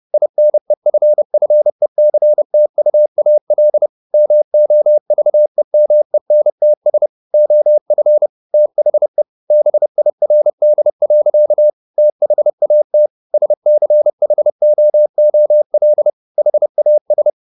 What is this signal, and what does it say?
Morse code, heard as a keyed tone, about 30 wpm, 600 Hz. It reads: INEFFECTUAL MOVEMENTS OF THE BIRD. THAT SCHOOL HAS
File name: the old man and the sea / the old man and the sea 1 30wpm